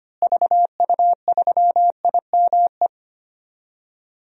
Morse code, heard as a keyed tone, about 25 wpm, 700 Hz. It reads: VU3IME
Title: DXCC-100 Call Signs - 6 Characters - Repeated 3x - 25wpm